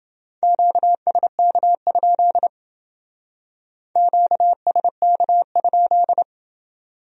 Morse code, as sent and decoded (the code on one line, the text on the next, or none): --.- ... -.- ..--.. / --.- ... -.- ..--..
QSK? QSK?